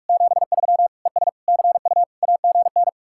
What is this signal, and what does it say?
Morse code, heard as a keyed tone, about 49 wpm, 700 Hz. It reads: 73 ES CU AGN